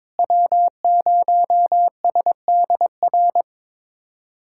Morse code, heard as a keyed tone, about 22 wpm, 700 Hz. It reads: W0SDR
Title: USA Callsign DE Callsign - 5 Characters - 22wpm